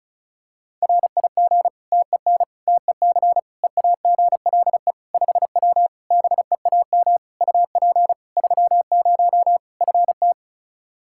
Morse code, heard as a keyed tone, about 36 wpm, 700 Hz. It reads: RIG TEN TEC EAGLE 5W BEAM UP 30 FT
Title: Practice QSO - Exchange 2 - Info - 35wpm